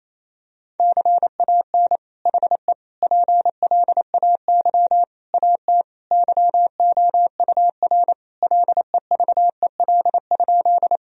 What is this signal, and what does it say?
Morse code, heard as a keyed tone, about 28 wpm, 700 Hz. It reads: CAN HE PLAY AT YOUR LEVEL?
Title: Sentences from Top 1000 Words - Review Part 4 - 28wpm